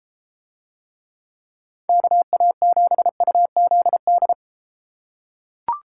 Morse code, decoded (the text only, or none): KA7UZD